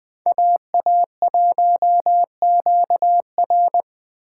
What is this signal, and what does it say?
Morse code, heard as a keyed tone, about 21 wpm, 700 Hz. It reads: AA1QR
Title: Speed Racing - USA 2x2 Call Signs 17wpm